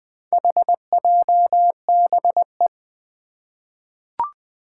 Morse code, heard as a keyed tone, about 20 wpm, 700 Hz. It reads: HJBE
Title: Four Challenging Letter-Number Combinations 20wpm